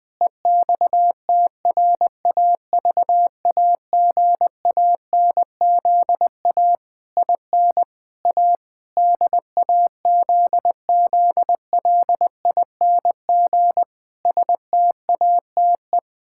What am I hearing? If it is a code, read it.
Morse code, heard as a keyed tone, about 20 wpm, 700 Hz. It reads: EXTRAVAGANZA IN A DAZZLING STATE
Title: Pangram Challenge v2 - 20wpm